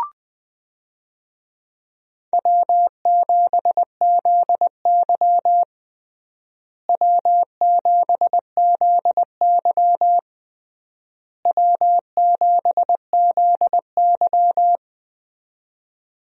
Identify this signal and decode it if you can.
Morse code, heard as a keyed tone, about 20 wpm, 700 Hz. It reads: W7ZY W7ZY W7ZY
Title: USA Call Signs - 4 Characters - Repeated 3x - 20wpm